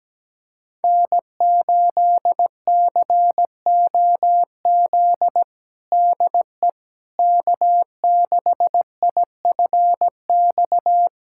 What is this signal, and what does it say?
Morse code, heard as a keyed tone, about 17 wpm, 700 Hz. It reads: N8COZ DE K6IFX
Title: USA Callsign DE Callsign - 5 Characters - 17wpm